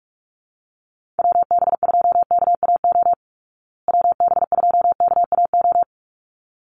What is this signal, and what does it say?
Morse code, heard as a keyed tone, about 44 wpm, 700 Hz. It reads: WB2KAO WB2KAO
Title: Call Signs Repeated 3x - 45wpm